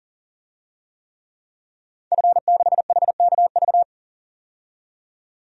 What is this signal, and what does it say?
Morse code, heard as a keyed tone, about 40 wpm, 700 Hz. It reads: F6HKV
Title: DXCC-100 Call Signs - 5 Characters - 40wpm